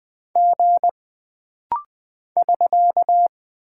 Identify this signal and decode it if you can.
Morse code, heard as a keyed tone, about 20 wpm, 700 Hz. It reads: G <SK>
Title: Single Character (Letter:Number:Punctuation:Prosign) - Rapid-Fire - 20wpm